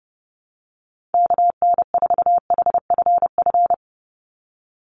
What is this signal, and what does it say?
Morse code, heard as a keyed tone, about 30 wpm, 700 Hz. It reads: KN4HFF
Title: USA Call Signs 30wpm